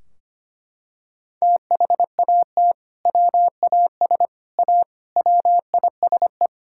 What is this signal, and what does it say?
Morse code, heard as a keyed tone, about 25 wpm, 700 Hz. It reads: THAT WAS A WISE
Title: Sentences from Top 1600 Words 25wpm